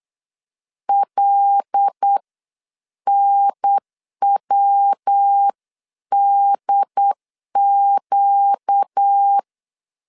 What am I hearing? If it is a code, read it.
Morse code, heard as a keyed tone, about 8 wpm, spaced as for 7 wpm, 800 Hz. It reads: LNWDQ
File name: 8wpm